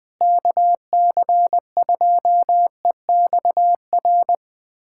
Morse code, heard as a keyed tone, about 20 wpm, 700 Hz. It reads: KC2EXR
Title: USA Callsign DE Callsign 20wpm